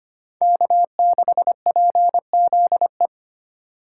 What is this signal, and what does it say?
Morse code, heard as a keyed tone, about 32 wpm, 700 Hz. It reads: K6PZE